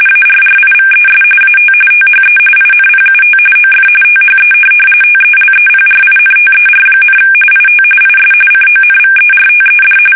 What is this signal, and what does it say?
RTTY, about 45.45 baud. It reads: ARRIVES AT THE INPUT OF THE TELEPRINTER, IT IS CONVERTED
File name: fsk850 / teletype-RTTY